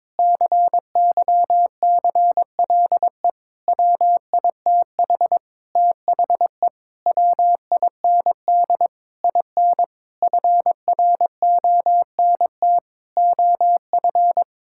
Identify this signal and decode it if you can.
Morse code, heard as a keyed tone, about 22 wpm, 700 Hz. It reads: CYCLE WITH THE WIND IN FRONT OF
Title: Sentences from Top 1000 Words - Review Part 6 - 22wpm